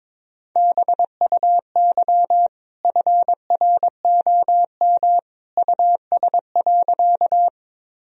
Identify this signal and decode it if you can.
Morse code, heard as a keyed tone, about 22 wpm, 700 Hz. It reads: BUY FROM US.
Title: Sentences from Top 1000 Words - Review Part 4 - 22wpm